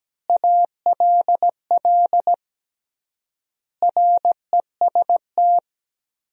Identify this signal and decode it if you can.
Morse code, heard as a keyed tone, about 17 wpm, 700 Hz. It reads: ALL REST
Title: 3x Word-Spacing - Sentences from Top 500 Words - 17wpm